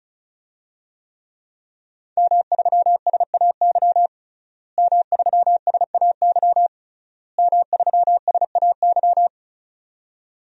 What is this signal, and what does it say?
Morse code, heard as a keyed tone, about 36 wpm, 700 Hz. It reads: M3SAY M3SAY M3SAY
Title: DXCC-100 Call Signs - 5 Characters - Repeated 3x - 35wpm